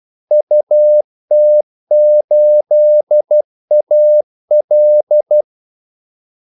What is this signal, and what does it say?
Morse code, heard as a keyed tone, about 12 wpm, 600 Hz. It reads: UT8AL
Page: 2 ekstra ordmellemrum mellem kaldesignalerne